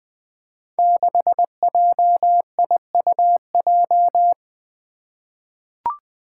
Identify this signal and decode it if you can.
Morse code, heard as a keyed tone, about 20 wpm, 700 Hz. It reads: 6JIUJ
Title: Five Challenging Letter-Number Combinations 20wpm